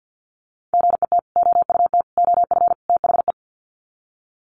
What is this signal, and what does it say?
Morse code, heard as a keyed tone, about 50 wpm, 700 Hz. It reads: GET OUT OF THE
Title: Sets of 4 Words 50wpm